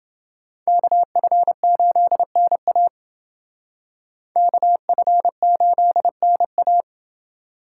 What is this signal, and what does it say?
Morse code, heard as a keyed tone, about 28 wpm, 700 Hz. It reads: KF8NA KF8NA